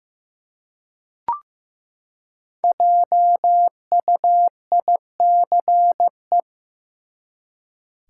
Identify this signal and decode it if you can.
Morse code, heard as a keyed tone, about 15 wpm, 700 Hz. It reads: JUICE
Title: Five Letter Words - 15wpm